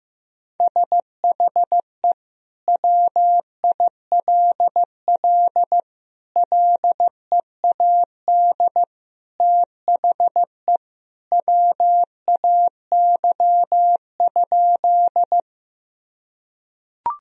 Morse code, heard as a keyed tone, about 15 wpm, 700 Hz. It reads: SHE WILL LEAD THE WAY?
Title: Sentences from Top 1000 Words - Review Part 2 - 15wpm